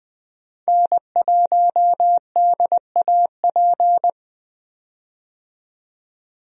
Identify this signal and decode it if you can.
Morse code, heard as a keyed tone, about 20 wpm, 700 Hz. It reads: N1DAP
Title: USA Call Signs - 5 Characters - 20wpm